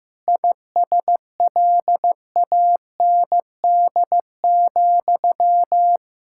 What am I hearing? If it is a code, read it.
Morse code, heard as a keyed tone, about 15 wpm, 700 Hz. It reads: ISLAND,
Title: Practice QSO - Exchange 1 - Beginning - 15wpm